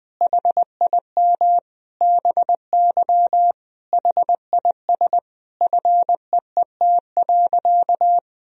Morse code, heard as a keyed tone, about 20 wpm, 700 Hz. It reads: HIM BY HIS FEET.
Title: Sentences from Top 1000 Words - Review Part 1 - 20wpm